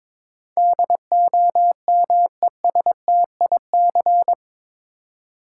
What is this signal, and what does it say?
Morse code, heard as a keyed tone, about 22 wpm, 700 Hz. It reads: DOMESTIC
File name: practice / 22wpm / Eight Letter Words - 22wpm